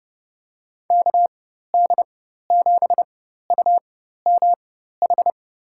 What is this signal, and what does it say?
Morse code, heard as a keyed tone, about 31 wpm, spaced as for 14 wpm, 700 Hz. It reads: KD7UMH
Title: USA Call Signs - 6 Characters - 4x Character Spacing - 30wpm